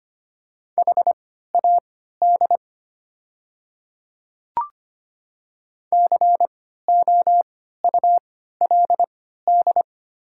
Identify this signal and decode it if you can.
Morse code, heard as a keyed tone, about 25 wpm, spaced as for 14 wpm, 700 Hz. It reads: HAD COULD
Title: 3x Character Spacing - Top 500 Words - 25wpm